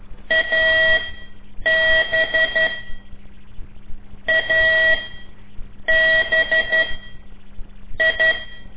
Morse code, heard as a keyed tone, about 12 wpm, spaced as for 7 wpm, 1950 Hz. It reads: AB ABI
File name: ik1weg2 / CORSI CW / lez5-1